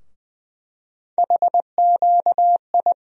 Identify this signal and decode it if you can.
Morse code, heard as a keyed tone, about 20 wpm, 700 Hz. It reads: HQI
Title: USA Call Sign Postfixes - 3 Letters 20wpm